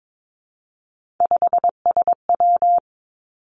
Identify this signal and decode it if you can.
Morse code, heard as a keyed tone, about 22 wpm, 700 Hz. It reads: 5SW
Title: Three Challenging Letter-Number Combinations 22wpm